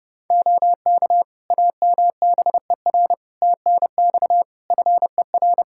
Morse code, heard as a keyed tone, about 28 wpm, 700 Hz. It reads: OK AMBER TNX FER
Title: Practice QSO - Exchange 4 - Ending - 30wpm